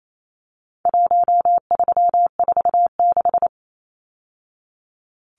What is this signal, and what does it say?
Morse code, heard as a keyed tone, about 28 wpm, 700 Hz. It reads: 1346
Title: Numbers - Contest Style - 4-Digit 28wpm